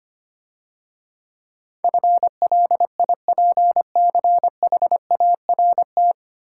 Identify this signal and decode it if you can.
Morse code, heard as a keyed tone, about 25 wpm, 700 Hz. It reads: FLIPCHART
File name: Easy Words - Speed-Racing with Hints - 25wpm